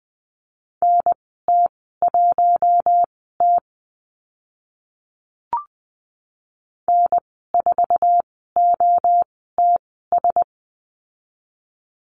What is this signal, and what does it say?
Morse code, heard as a keyed tone, about 20 wpm, spaced as for 15 wpm, 700 Hz. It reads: NT1T N4OTS
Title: 048 Lesson 9g - Number 4 - Call Signs 20wpm